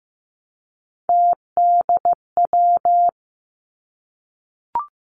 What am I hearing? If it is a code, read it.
Morse code, heard as a keyed tone, about 15 wpm, 700 Hz. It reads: TDW
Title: Three Challenging Letter-Number Combinations 15wpm